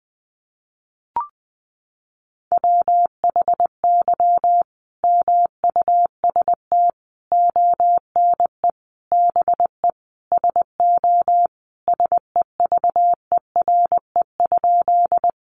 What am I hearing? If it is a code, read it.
Morse code, heard as a keyed tone, about 20 wpm, 700 Hz. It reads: WHY MUST ONE BE SO SEVERE?
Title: Sentences from Top 1400 Words 20wpm